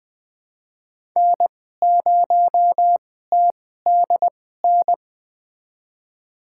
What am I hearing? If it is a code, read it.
Morse code, heard as a keyed tone, about 20 wpm, spaced as for 14 wpm, 700 Hz. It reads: N0TDN
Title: USA Call Signs - 5 Characters - 2x Character Spacing - 20wpm